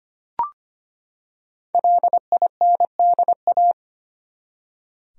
Morse code, heard as a keyed tone, about 25 wpm, 700 Hz. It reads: LINDA
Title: USA Common Names - Max 5 Letters - 25wpm